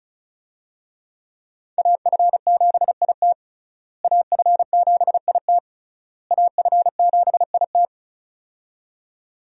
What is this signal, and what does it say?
Morse code, heard as a keyed tone, about 36 wpm, 700 Hz. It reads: AF7IT AF7IT AF7IT
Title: USA Call Signs - 5 Characters - Repeated 3x - 35wpm